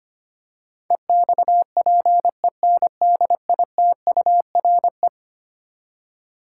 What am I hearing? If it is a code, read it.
Morse code, heard as a keyed tone, about 25 wpm, 700 Hz. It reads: EXPENDITURE
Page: morse-code-ninja-practice-25wpm